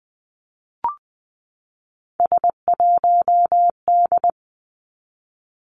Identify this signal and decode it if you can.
Morse code, heard as a keyed tone, about 20 wpm, 700 Hz. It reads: S1D
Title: Three Challenging Letter-Number Combinations 20wpm